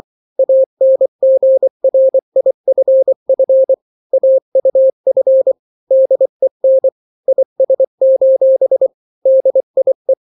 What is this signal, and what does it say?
Morse code, heard as a keyed tone, about 24 wpm, 530 Hz. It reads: ANGRIFF AUF DEN IS: DIE
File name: daily / 23wpm